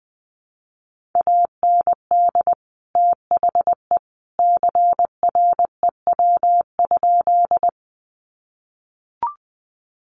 Morse code, heard as a keyed tone, about 20 wpm, 700 Hz. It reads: AND THE CREW?
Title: Sentences from Top 1500 Words 20wpm